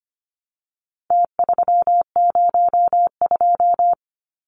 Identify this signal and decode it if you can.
Morse code, heard as a keyed tone, about 25 wpm, 700 Hz. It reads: T302
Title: Numbers - Contest Style - 4-Digit 25wpm